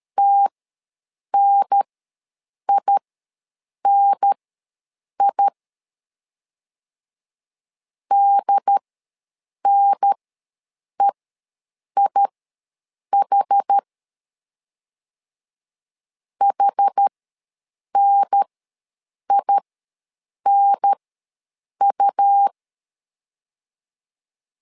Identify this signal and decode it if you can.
Morse code, heard as a keyed tone, about 13 wpm, spaced as for 7 wpm, 800 Hz. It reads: TNINI DNEIH HNINU
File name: LEZIONI / 2 UHDN / 2 UHDN ws 12wpm 1708